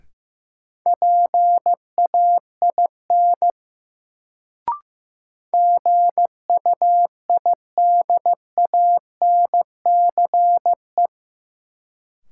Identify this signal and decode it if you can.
Morse code, heard as a keyed tone, about 15 wpm, 700 Hz. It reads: PAIN GUIDANCE
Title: Sentences from Top 1600 Words - New Word Drill - 15wpm